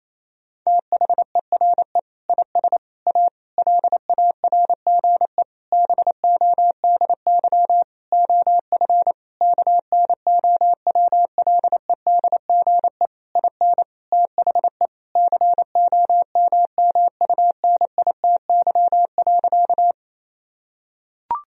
Morse code, heard as a keyed tone, about 28 wpm, 700 Hz. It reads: THERE IS A LARGE BODY OF KNOWLEDGE IN THE COMMUNITY.
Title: Sentences from Top 1000 Words - Review Part 4 - 28wpm